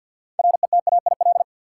Morse code, heard as a keyed tone, about 48 wpm, 700 Hz. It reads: RETAIL
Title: Practice QSO - Exchange 3 - More Info - 50wpm